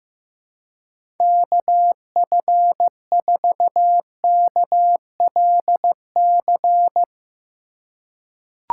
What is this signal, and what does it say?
Morse code, heard as a keyed tone, about 15 wpm, 700 Hz. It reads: KF4KLC
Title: USA Call Signs - 6 Characters - 4x Character Spacing - 15wpm